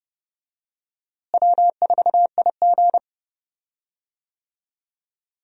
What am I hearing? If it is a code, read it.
Morse code, heard as a keyed tone, about 31 wpm, 700 Hz. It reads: W4IG
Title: USA Call Signs - 4 Characters - Repeated 3x - 30wpm